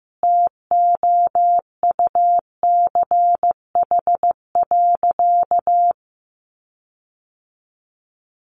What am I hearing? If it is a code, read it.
Morse code, heard as a keyed tone, about 15 wpm, 700 Hz. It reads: TOUCH.
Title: Sentences from Top 1500 Words 15wpm